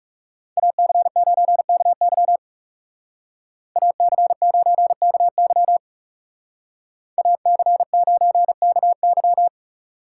Morse code, heard as a keyed tone, about 39 wpm, 700 Hz. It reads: AC9KY AC9KY AC9KY